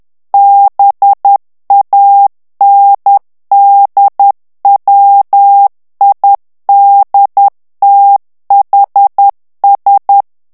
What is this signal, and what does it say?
Morse code, heard as a keyed tone, about 10 wpm, 800 Hz. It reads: BANDWIDTHS
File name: ANGLAIS texte clair 3 vitesse 10 mots minute